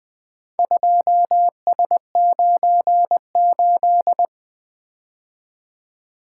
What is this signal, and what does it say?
Morse code, heard as a keyed tone, about 20 wpm, 700 Hz. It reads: 2S98
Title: Four Letter-Number Combinations - 20wpm